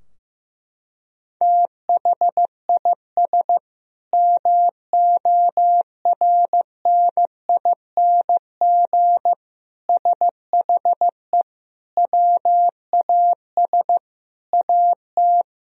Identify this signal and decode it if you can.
Morse code, heard as a keyed tone, about 15 wpm, 700 Hz. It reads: THIS MORNING SHE WAS AT
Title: Sentences from Top 1000 Words - Review Part 3 - 15wpm